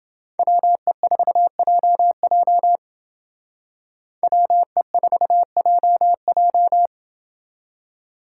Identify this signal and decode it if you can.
Morse code, heard as a keyed tone, about 30 wpm, 700 Hz. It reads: WE4JJ WE4JJ